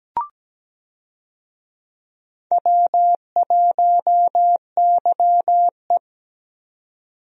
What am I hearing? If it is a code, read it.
Morse code, heard as a keyed tone, about 17 wpm, 700 Hz. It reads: W1YE
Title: USA Call Signs - 4 Characters - 17wpm